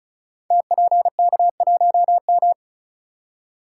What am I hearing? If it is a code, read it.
Morse code, heard as a keyed tone, about 36 wpm, 700 Hz. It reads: TPK1M
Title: Code Groups - 5 Letter:Numbers - 35wpm